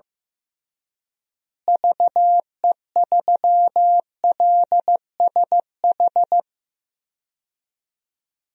Morse code, heard as a keyed tone, about 15 wpm, 700 Hz. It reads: VE3LSH
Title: DXCC-100 Call Signs - 6 Characters - Repeated 3x - 15wpm